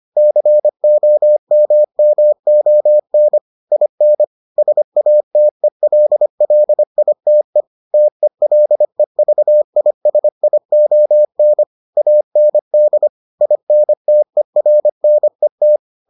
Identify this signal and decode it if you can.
Morse code, heard as a keyed tone, about 25 wpm, 600 Hz. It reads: COMMON IN SATELLITE TELEVISION AND INTERNET